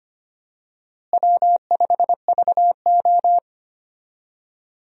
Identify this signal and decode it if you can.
Morse code, heard as a keyed tone, about 25 wpm, 700 Hz. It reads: W5VO
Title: USA Call Signs - 4 Characters - Repeated 2x - 25wpm